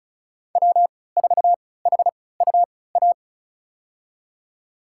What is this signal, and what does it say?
Morse code, heard as a keyed tone, about 36 wpm, spaced as for 20 wpm, 690 Hz. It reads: W4HUA
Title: USA Call Signs - 3x Character Spacing - 35wpm